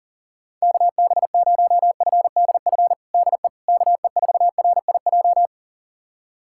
Morse code, heard as a keyed tone, about 40 wpm, 700 Hz. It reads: KB0FDF DE KE4RIJ
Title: USA Callsign DE Callsign 40wpm